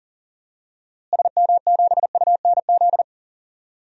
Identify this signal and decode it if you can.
Morse code, heard as a keyed tone, about 40 wpm, 700 Hz. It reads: SM7UNZ
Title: DXCC-100 Call Signs - 6 Characters - Repeated 2x - 40wpm